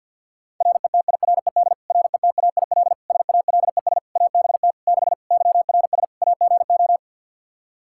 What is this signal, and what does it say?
Morse code, heard as a keyed tone, about 46 wpm, 700 Hz. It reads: RETIRED RETAIL SALES ABT 6 YRS AGO
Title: Practice QSO - Exchange 3 - More Info - 50wpm